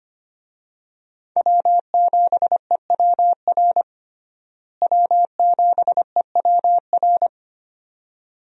Morse code, heard as a keyed tone, about 25 wpm, 700 Hz. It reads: W7EWR W7EWR
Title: USA Call Signs - 5 Characters - Repeated 2x - 25wpm